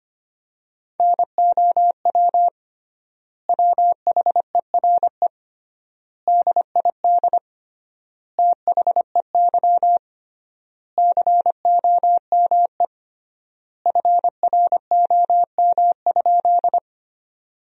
Morse code, heard as a keyed tone, about 25 wpm, 700 Hz. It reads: NOW WHERE DID THEY COME FROM?
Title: 3x Word-Spacing - Sentences from Top 200 Words - 25wpm